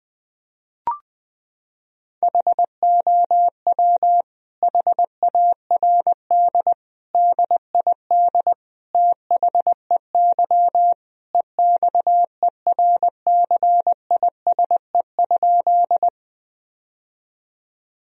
Morse code, heard as a keyed tone, about 20 wpm, 700 Hz. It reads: HOW HARD DID THEY EXERCISE?
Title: Sentences from Top 1000 Words - Review Part 4 - 20wpm